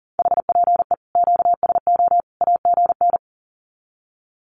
Morse code, heard as a keyed tone, about 40 wpm, 700 Hz. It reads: HPE QSO AGN
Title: Common QSO Phrases 40wpm